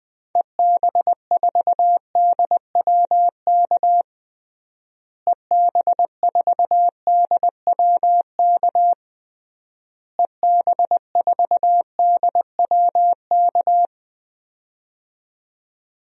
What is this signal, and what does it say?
Morse code, heard as a keyed tone, about 20 wpm, 700 Hz. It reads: EB4DWK EB4DWK EB4DWK
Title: DXCC-100 Call Signs - 6 Characters - Repeated 3x - 20wpm